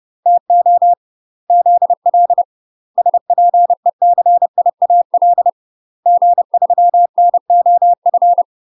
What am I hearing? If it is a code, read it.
Morse code, heard as a keyed tone, about 30 wpm, 700 Hz. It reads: TO ZL SPECIAL G3NOF
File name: UK MASTER QSOs TRAINING / 30WPM / UK MASTER QSOs 0024